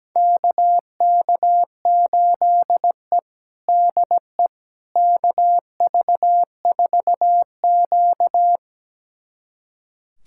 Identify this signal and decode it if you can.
Morse code, heard as a keyed tone, about 17 wpm, 700 Hz. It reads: KK8E DE KV4Q
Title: USA Callsign DE Callsign - 4 Characters - 17wpm